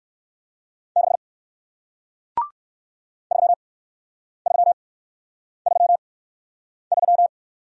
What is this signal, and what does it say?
Morse code, heard as a keyed tone, about 56 wpm, 700 Hz. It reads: 6 3 3 3 3
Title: Speed-Racing Single Letter-Number 45wpm